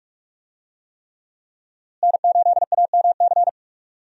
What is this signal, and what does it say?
Morse code, heard as a keyed tone, about 47 wpm, 700 Hz. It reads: N8AMC
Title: USA Call Signs - 5 Characters - Repeated 3x - 45wpm